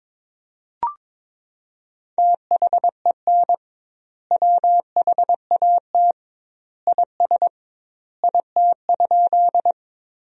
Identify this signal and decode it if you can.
Morse code, heard as a keyed tone, about 22 wpm, 700 Hz. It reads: THEN WHAT IS IT?
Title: 2x Word-Spacing - Sentences from Top 100 Words - 22wpm